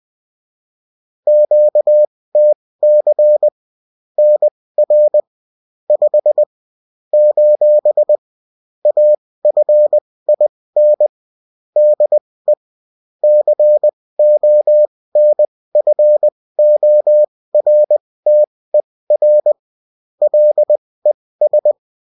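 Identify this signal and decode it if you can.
Morse code, heard as a keyed tone, about 20 wpm, spaced as for 16 wpm, 600 Hz. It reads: QTC NR 5 : AFIN DE CONFORTER LES
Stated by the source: Morse code bulletin of the UFT.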